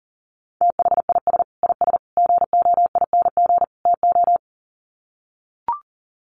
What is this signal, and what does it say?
Morse code, heard as a keyed tone, about 40 wpm, 700 Hz. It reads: THIS IS GOING TO
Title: Sets of 4 Words 40wpm